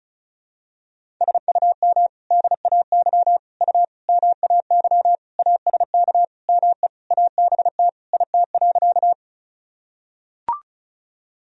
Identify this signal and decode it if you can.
Morse code, heard as a keyed tone, about 34 wpm, 700 Hz. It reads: SUM DAY U MAY ASK ME ABT IT.
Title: Sentences from Top 200 Words and Abbreviations 35wpm